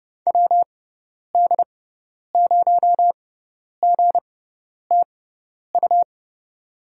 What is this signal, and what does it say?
Morse code, heard as a keyed tone, about 30 wpm, spaced as for 10 wpm, 700 Hz. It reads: WD0GTU
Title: USA Call Signs - 6 Characters - 6x Character Spacing - 30wpm